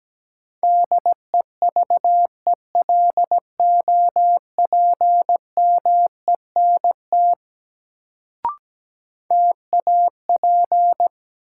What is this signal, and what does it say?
Morse code, heard as a keyed tone, about 17 wpm, 700 Hz. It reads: DEVELOPMENT TAP
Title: Sentences from Top 1400 Words - New Word Drill Alt - 17wpm